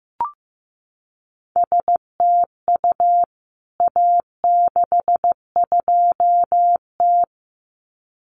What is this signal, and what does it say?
Morse code, heard as a keyed tone, about 15 wpm, 700 Hz. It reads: STU A62T
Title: CWT Exchange 15wpm